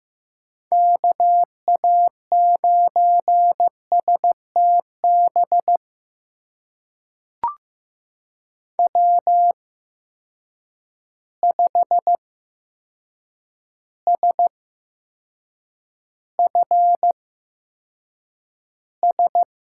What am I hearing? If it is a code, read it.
Morse code, heard as a keyed tone, about 15 wpm, spaced as for 7 wpm, 700 Hz. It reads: KA9STB W5SFS
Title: USA Call Signs - 8x Character Spacing - 15wpm